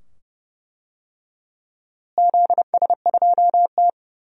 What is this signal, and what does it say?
Morse code, heard as a keyed tone, about 32 wpm, 700 Hz. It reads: ZS2T
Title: DXCC-100 Call Signs - 4 Characters - Repeated 2x - 30wpm